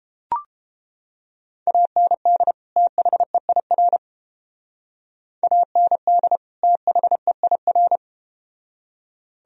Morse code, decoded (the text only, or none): AND THEIR AND THEIR